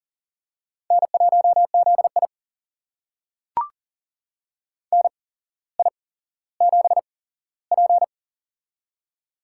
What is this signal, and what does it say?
Morse code, heard as a keyed tone, about 40 wpm, spaced as for 17 wpm, 700 Hz. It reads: N1ZI NI7P
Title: USA Call Signs - 4 Characters - 8x Character Spacing - 40wpm